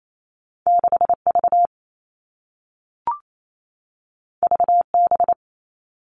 Morse code, heard as a keyed tone, about 28 wpm, 700 Hz. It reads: 6 V V B
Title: Two Challenging Letter-Number Combinations 28wpm